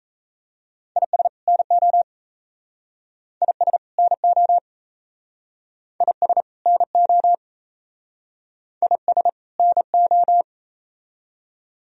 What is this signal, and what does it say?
Morse code, heard as a keyed tone, about 36 wpm, 700 Hz. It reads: IS NO IS NO IS NO IS NO